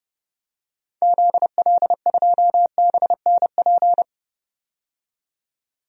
Morse code, heard as a keyed tone, about 31 wpm, 700 Hz. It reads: ZL2BNP
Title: DXCC-100 Call Signs - 6 Characters - 30wpm